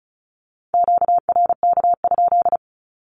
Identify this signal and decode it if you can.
Morse code, heard as a keyed tone, about 34 wpm, 700 Hz. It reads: QRK?
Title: Q-Codes with Spoken Meaning - 35wpm